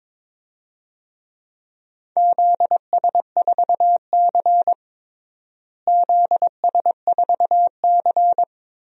Morse code, heard as a keyed tone, about 22 wpm, 700 Hz. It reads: ZS4C ZS4C
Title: DXCC-100 Call Signs - 4 Characters - Repeated 3x - 22wpm